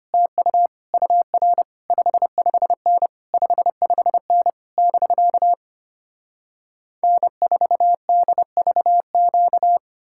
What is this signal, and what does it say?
Morse code, heard as a keyed tone, about 33 wpm, 700 Hz. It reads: TU UR 55N 55N <BK> N4DVQ